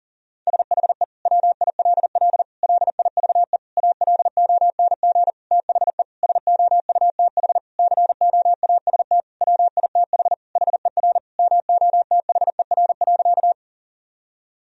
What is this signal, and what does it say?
Morse code, heard as a keyed tone, about 40 wpm, 700 Hz. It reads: SHE WILL LIVE ALONG THE SOUTH COAST WITH HER MOTHER.
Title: Sentences from Top 1000 Words - Review Part 4 - 40wpm